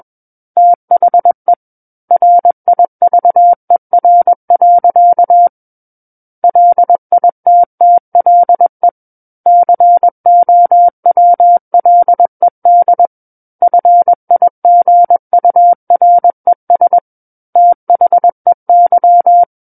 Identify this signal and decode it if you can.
Morse code, heard as a keyed tone, about 21 wpm, 700 Hz. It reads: THE RIVER. LITTLE COWLED FIGURES THEY
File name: Book One / 12-Chapter 12 - 21 WPM